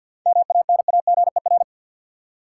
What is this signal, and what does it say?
Morse code, heard as a keyed tone, about 52 wpm, 700 Hz. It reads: MANAGER
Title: Seven Letter Words - 50wpm